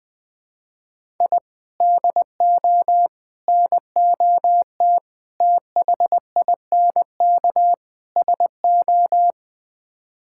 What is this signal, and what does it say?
Morse code, heard as a keyed tone, about 20 wpm, 700 Hz. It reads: I DO NOT THINK SO
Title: Build A Phrase - 5-Word Phrases - 20wpm